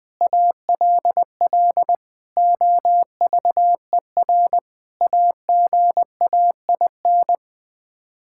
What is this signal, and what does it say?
Morse code, heard as a keyed tone, about 19 wpm, 700 Hz. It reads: ALL OVER AGAIN
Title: Sets of 9 Words - 20wpm